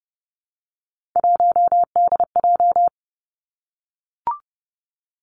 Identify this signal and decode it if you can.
Morse code, heard as a keyed tone, about 30 wpm, 700 Hz. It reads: 1DJ
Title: Three Challenging Letter-Number Combinations 30wpm